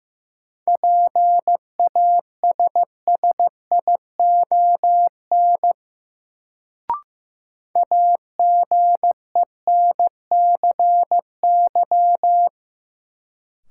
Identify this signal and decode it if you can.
Morse code, heard as a keyed tone, about 15 wpm, 700 Hz. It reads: PASSION AGENCY
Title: Sentences from Top 1300 Words - New Word Drill Alt - 15wpm